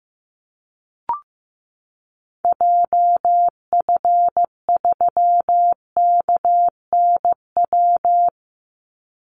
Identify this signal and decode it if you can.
Morse code, heard as a keyed tone, about 15 wpm, 700 Hz. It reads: JF3KNW
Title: Call Signs Repeated 3x - 15wpm